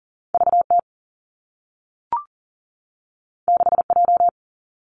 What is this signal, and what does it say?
Morse code, heard as a keyed tone, about 40 wpm, 700 Hz. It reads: V T 6 J
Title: Two Challenging Letter-Number Combinations 40wpm